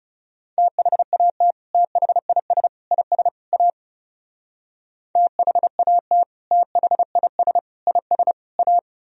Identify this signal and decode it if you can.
Morse code, heard as a keyed tone, about 35 wpm, 700 Hz. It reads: THAT THIS IS A THAT THIS IS A
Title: Speed-Racing - Sets of 4 Words 30wpm